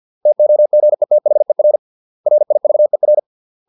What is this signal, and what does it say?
Morse code, heard as a keyed tone, about 50 wpm, 600 Hz. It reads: TOGETHER RIVER